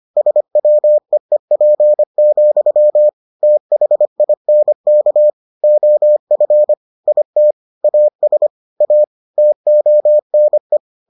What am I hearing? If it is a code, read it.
Morse code, heard as a keyed tone, about 25 wpm, 600 Hz. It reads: SWEEP, THINK OF IT AS A TONE